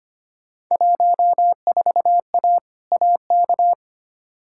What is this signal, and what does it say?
Morse code, heard as a keyed tone, about 25 wpm, 700 Hz. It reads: 14A AK
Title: Field Day Exchanges Encore - 25wpm